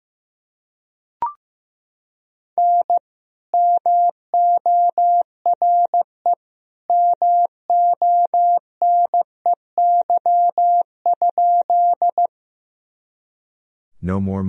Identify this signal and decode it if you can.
Morse code, heard as a keyed tone, about 15 wpm, 700 Hz. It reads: N MORE MONEY?
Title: Sentences from Top 400 Words and Abbreviations 15wpm